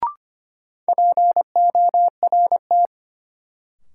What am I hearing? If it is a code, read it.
Morse code, heard as a keyed tone, about 25 wpm, 700 Hz. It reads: PORT
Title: Sentences from Top 1000 Words - Review Part 1 - New Word Drill - 25wpm